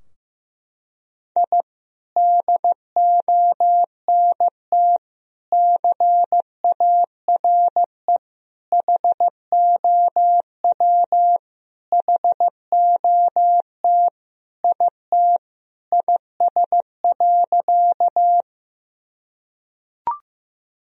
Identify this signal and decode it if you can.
Morse code, heard as a keyed tone, about 15 wpm, 700 Hz. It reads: I DONT CARE HOW HOT IT IS.
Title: Sentences from Top 1000 Words - Review Part 3 - 15wpm